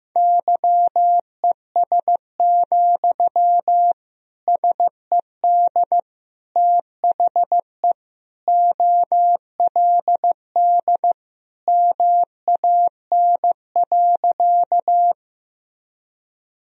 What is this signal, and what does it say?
Morse code, heard as a keyed tone, about 15 wpm, 700 Hz. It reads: YES, SED THE OLD MAN.
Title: Sentences from Top 500 Words and Abbreviations 15wpm